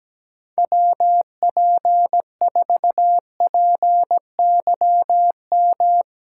Morse code, heard as a keyed tone, about 17 wpm, 700 Hz. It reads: WP4PYM
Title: USA Callsign DE Callsign - 6 Characters - 17wpm